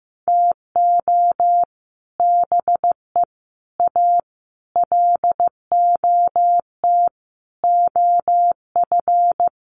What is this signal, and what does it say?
Morse code, heard as a keyed tone, about 15 wpm, 700 Hz. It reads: TO BE A LOT OF
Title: Sets of 5 Words Encore 15wpm